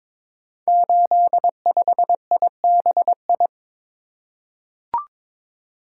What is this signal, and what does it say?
Morse code, heard as a keyed tone, about 22 wpm, 700 Hz. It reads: 85IBI
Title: Code Groups - 5 Letter:Numbers - 22wpm